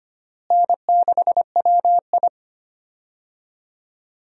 Morse code, heard as a keyed tone, about 25 wpm, 700 Hz. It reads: N6WI
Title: USA Call Signs - 4 Characters - Repeated 3x - 25wpm